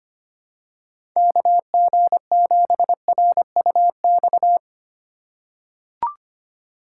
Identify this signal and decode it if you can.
Morse code, heard as a keyed tone, about 25 wpm, 700 Hz. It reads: KG7RUX
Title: USA Call Signs - 6 Characters - 3x Character Spacing - 25wpm